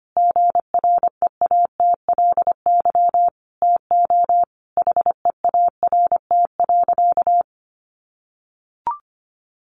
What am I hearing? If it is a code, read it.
Morse code, heard as a keyed tone, about 25 wpm, 700 Hz. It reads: GREATLY TO HEART.
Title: Sentences from Top 1400 Words 25wpm